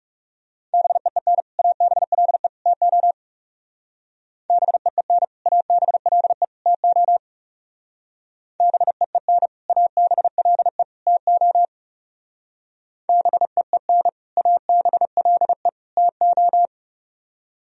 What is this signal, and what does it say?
Morse code, heard as a keyed tone, about 37 wpm, 700 Hz. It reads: BEEN ABLE TO BEEN ABLE TO BEEN ABLE TO BEEN ABLE TO
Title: Speed-Racing - Sets of 3 Words 30wpm